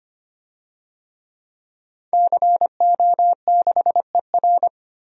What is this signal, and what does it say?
Morse code, heard as a keyed tone, about 25 wpm, 700 Hz. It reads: CO6ER
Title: DXCC-100 Call Signs - 5 Characters - Repeated 3x - 25wpm